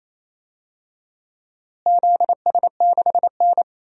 Morse code, heard as a keyed tone, about 28 wpm, 700 Hz. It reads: ZS6N
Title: DXCC-100 Call Signs - 4 Characters - Repeated 3x - 28wpm